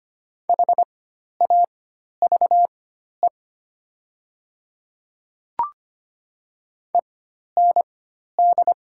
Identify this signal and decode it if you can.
Morse code, heard as a keyed tone, about 25 wpm, spaced as for 12 wpm, 700 Hz. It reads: HAVE END
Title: 4x Character Spacing - Top 200 Words - 25wpm